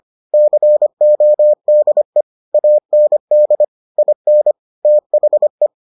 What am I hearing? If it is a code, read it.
Morse code, heard as a keyed tone, about 25 wpm, 600 Hz. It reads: CODE AND IN THE